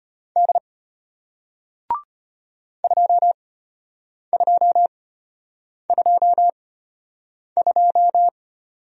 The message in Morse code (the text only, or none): D 2222